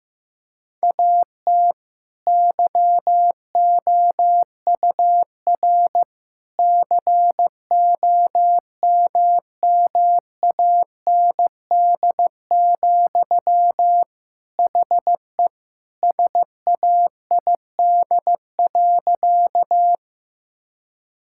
Morse code, heard as a keyed tone, about 15 wpm, 700 Hz. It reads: AT YOUR COMMAND, HE SAID.
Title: Sentences from Top 1600 Words 15wpm